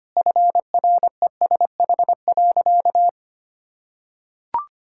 Morse code, heard as a keyed tone, about 25 wpm, 700 Hz. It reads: FRESH.
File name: practice / 25wpm / Sentences with 2-Word Phrases - 25wpm